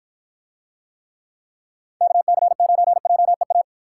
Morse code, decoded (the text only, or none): KC9JEA